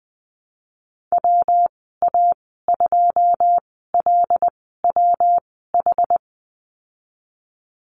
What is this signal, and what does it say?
Morse code, heard as a keyed tone, about 20 wpm, spaced as for 15 wpm, 700 Hz. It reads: WA2LWH
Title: 147 Lesson 24g - Letter y - Call Signs 20wpm